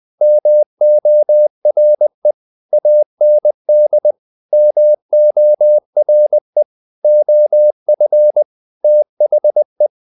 Morse code, heard as a keyed tone, about 20 wpm, 600 Hz. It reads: MORE AND MORE OF THE
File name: the old man and the sea part 2 20wpm